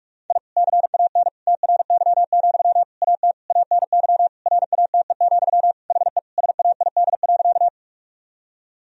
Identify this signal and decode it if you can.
Morse code, heard as a keyed tone, about 43 wpm, 700 Hz. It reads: I CAN TRY, AT ANY RATE, HE SAID.
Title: Sentences from Top 1000 Words - Review Part 4 - 45wpm